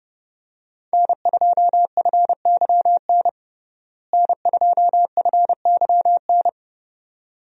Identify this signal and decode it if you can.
Morse code, heard as a keyed tone, about 30 wpm, 700 Hz. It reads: N2FYN N2FYN
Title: USA Call Signs - 5 Characters - Repeated 2x - 30wpm